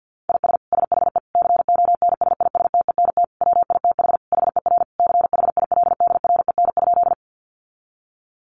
Morse code, heard as a keyed tone, about 48 wpm, 700 Hz. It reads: IS SHE CONSISTENT WITH HER CHILDREN?
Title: Sentences from Top 1500 Words 50wpm